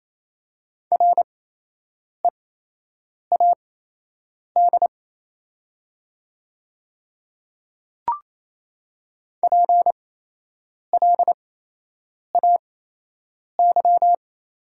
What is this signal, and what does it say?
Morse code, heard as a keyed tone, about 28 wpm, spaced as for 8 wpm, 700 Hz. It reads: READ PLAY
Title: 8x Character Spacing - Top 200 Words - 28wpm